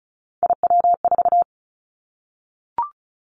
Morse code, heard as a keyed tone, about 36 wpm, 700 Hz. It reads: IW4
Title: Callsign Prefixes for Top 50 DXCC - 35wpm